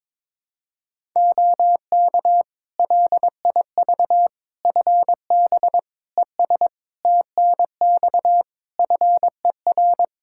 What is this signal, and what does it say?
Morse code, heard as a keyed tone, about 22 wpm, 700 Hz. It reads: OK LIV FB ES TNX FER
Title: Practice QSO - Exchange 2 - Info - 22wpm